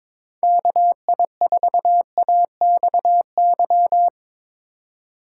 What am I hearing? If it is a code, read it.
Morse code, heard as a keyed tone, about 22 wpm, 700 Hz. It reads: KI4AXY
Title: USA Call Signs - 2x Character Spacing - 22wpm